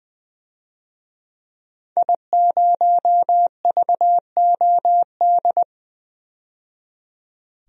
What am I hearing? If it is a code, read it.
Morse code, heard as a keyed tone, about 20 wpm, 700 Hz. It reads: I0VOD
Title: DXCC-100 Call Signs - 5 Characters - 20wpm